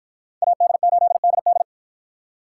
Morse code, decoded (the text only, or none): AD8DD